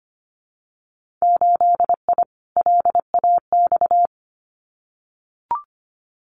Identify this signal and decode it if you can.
Morse code, heard as a keyed tone, about 25 wpm, 700 Hz. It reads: 8I LAX
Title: Winter Field Day Exchanges Encore 25wpm